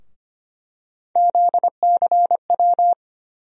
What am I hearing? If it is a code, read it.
Morse code, heard as a keyed tone, about 25 wpm, 700 Hz. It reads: ZCW
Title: USA Call Sign Postfixes - 3 Letters 25wpm